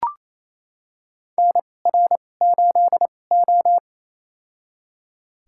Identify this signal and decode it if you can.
Morse code, heard as a keyed tone, about 28 wpm, spaced as for 20 wpm, 700 Hz. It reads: NR8O
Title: USA Call Signs - 4 Characters - 2x Character Spacing - 28wpm